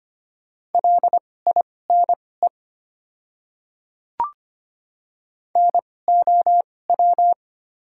Morse code, heard as a keyed tone, about 25 wpm, spaced as for 18 wpm, 700 Hz. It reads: LINE NOW
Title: 2x Character Spacing - Top 200 Words - 25wpm